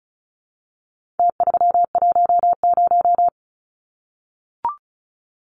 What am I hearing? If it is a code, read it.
Morse code, heard as a keyed tone, about 34 wpm, 700 Hz. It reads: T310
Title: Numbers - Contest Style - 4-Digit 35wpm